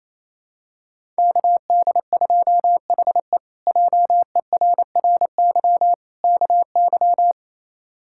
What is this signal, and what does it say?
Morse code, heard as a keyed tone, about 27 wpm, 700 Hz. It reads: KD2HE JERRY KY
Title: K1USN SST QSO Practice - 28wpm